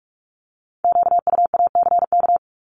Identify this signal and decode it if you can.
Morse code, heard as a keyed tone, about 46 wpm, 700 Hz. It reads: QUACK
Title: Multi-letter Phonemes 45wpm